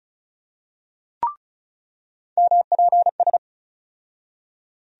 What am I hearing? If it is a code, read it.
Morse code, heard as a keyed tone, about 34 wpm, 700 Hz. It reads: MPS
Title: USA Call Sign Postfixes - 3 Letters 35wpm